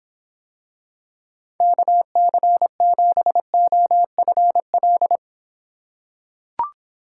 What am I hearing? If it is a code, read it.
Morse code, heard as a keyed tone, about 26 wpm, 700 Hz. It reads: KC7OFL